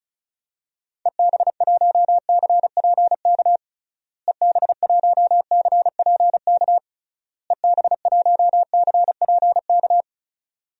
Morse code, heard as a keyed tone, about 34 wpm, 700 Hz. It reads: EB1CPK EB1CPK EB1CPK
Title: DXCC-100 Call Signs - 6 Characters - Repeated 3x - 35wpm